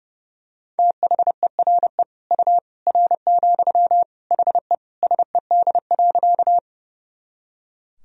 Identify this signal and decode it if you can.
Morse code, heard as a keyed tone, about 30 wpm, 700 Hz. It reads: THERE U R, HE SED.
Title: Sentences from Top 100 Words and Abbreviations 30wpm